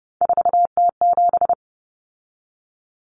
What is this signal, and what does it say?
Morse code, heard as a keyed tone, about 30 wpm, 700 Hz. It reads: 4T7
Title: Common Cut Numbers - 3-Digit 30wpm